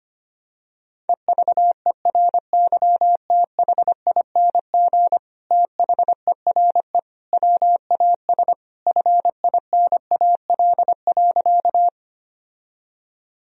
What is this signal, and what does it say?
Morse code, heard as a keyed tone, about 25 wpm, 700 Hz. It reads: EVERYTHING THERE WAS FINAL.
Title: Sentences from Top 1000 Words - Review Part 2 - 25wpm